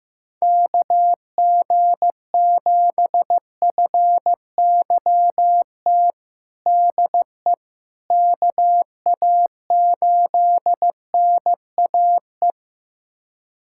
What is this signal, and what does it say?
Morse code, heard as a keyed tone, about 15 wpm, 700 Hz. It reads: KG7FYT DE KA8NAE
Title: USA Callsign DE Callsign - 6 Characters - 15wpm